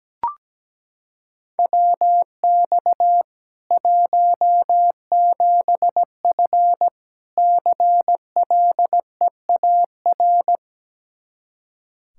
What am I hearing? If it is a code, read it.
Morse code, heard as a keyed tone, about 17 wpm, 700 Hz. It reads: WX 17F CLEAR
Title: Practice QSO - Exchange 2 - Info - 17wpm